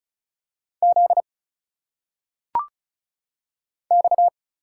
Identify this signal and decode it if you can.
Morse code, heard as a keyed tone, about 36 wpm, 700 Hz. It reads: Z X
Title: Single Letters - Phonetic Pronunciation 35wpm